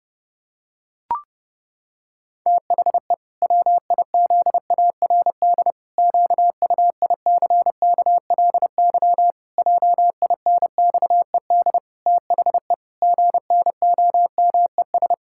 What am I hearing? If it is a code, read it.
Morse code, heard as a keyed tone, about 32 wpm, 700 Hz. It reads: THE WIZARD QUICKLY JINXED THE GNOMES
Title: Pangram Challenge v2 - 30wpm